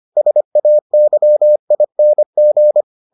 Morse code, heard as a keyed tone, about 25 wpm, 600 Hz. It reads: SAYING